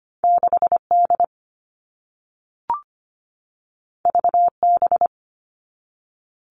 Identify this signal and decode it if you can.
Morse code, heard as a keyed tone, about 25 wpm, 700 Hz. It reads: 6D VB
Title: All Cut Numbers - 2-Digit 25wpm